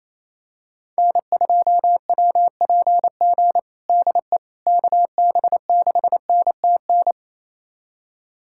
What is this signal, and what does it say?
Morse code, heard as a keyed tone, about 28 wpm, 700 Hz. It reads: N2WPG DE KB6NTN
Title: USA Callsign DE Callsign 28wpm